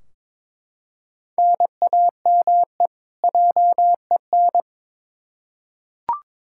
Morse code, heard as a keyed tone, about 22 wpm, 700 Hz. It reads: NAME JEN
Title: K1USN SST QSO Practice 22wpm